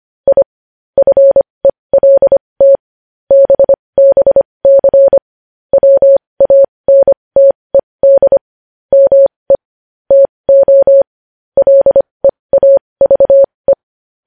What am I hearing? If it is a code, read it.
Morse code, heard as a keyed tone, about 25 wpm, spaced as for 20 wpm, 550 Hz. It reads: I FELT BBC WANTED ME TO LEAVE
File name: podcast / News headlines-1x-25x20